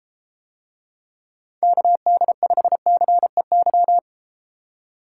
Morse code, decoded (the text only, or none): KD5CEY